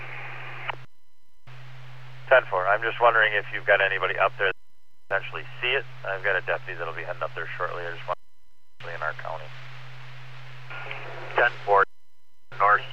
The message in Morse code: none

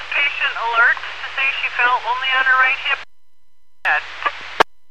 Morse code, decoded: none

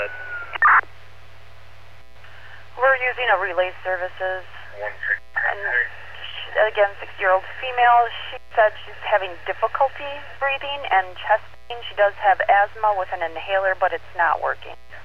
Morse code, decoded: none

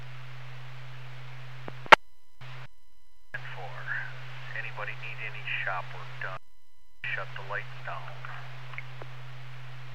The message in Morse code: none

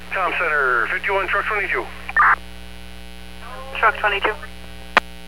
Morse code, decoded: none

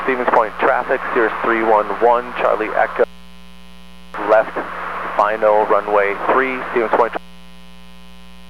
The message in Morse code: none

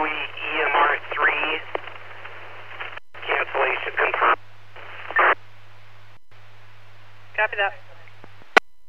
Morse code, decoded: none

